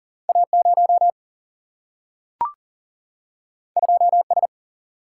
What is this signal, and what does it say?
Morse code, heard as a keyed tone, about 40 wpm, 700 Hz. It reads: A 0 2 S
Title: Single-Letter Single-Number 40wpm